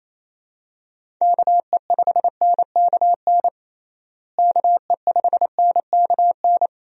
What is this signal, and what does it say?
Morse code, heard as a keyed tone, about 27 wpm, 700 Hz. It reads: KE5NKN KE5NKN
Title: USA Call Signs - 6 Characters - Repeated 3x - 28wpm